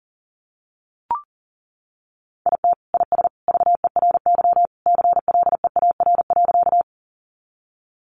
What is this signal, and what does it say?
Morse code, heard as a keyed tone, about 37 wpm, 700 Hz. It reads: IT IS VERY CLEAR.
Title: Sentences from Top 500 Words 40wpm